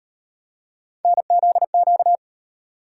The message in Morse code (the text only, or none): NZQ